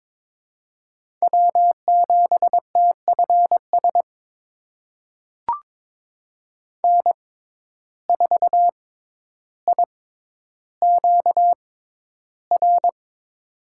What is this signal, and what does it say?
Morse code, heard as a keyed tone, about 22 wpm, spaced as for 11 wpm, 700 Hz. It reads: W7TFS N4IQR
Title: USA Call Signs - 6x Character Spacing - 22wpm